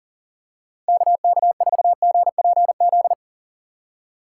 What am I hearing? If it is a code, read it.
Morse code, heard as a keyed tone, about 37 wpm, 700 Hz. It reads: KK4GPZ
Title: USA Call Signs - 6x Character Spacing - 40wpm